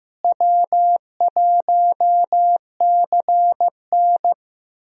Morse code, heard as a keyed tone, about 15 wpm, 700 Hz. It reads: W1CN
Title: USA Call Signs - 4 Characters - Repeated 3x - 15wpm